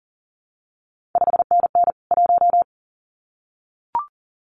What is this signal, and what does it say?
Morse code, heard as a keyed tone, about 42 wpm, 700 Hz. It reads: 5NN 1
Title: CQ WW DX Contest Exchange Encore - 40wpm